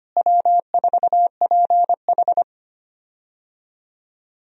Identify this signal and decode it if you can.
Morse code, heard as a keyed tone, about 25 wpm, 700 Hz. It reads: W4PH
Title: USA Call Signs - 4 Characters - Repeated 2x - 25wpm